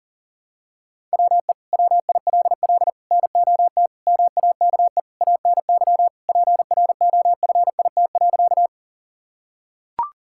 Morse code, heard as a keyed tone, about 37 wpm, 700 Hz. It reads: WE WILL NOT MAKE ANY PROFIT.
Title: Sentences from Top 1000 Words - Review Part 5 - 40wpm